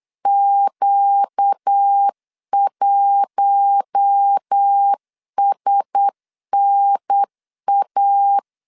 Morse code, heard as a keyed tone, about 8 wpm, 800 Hz. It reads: Q1SNA
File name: LEZIONI / 10 SIGNS / 10A SLASH 8wpm 3452